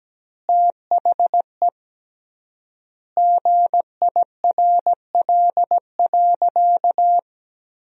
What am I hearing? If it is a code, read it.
Morse code, heard as a keyed tone, about 17 wpm, 700 Hz. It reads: THE GIRL.
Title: 3x Word-Spacing - Sentences from Top 400 Words - 17wpm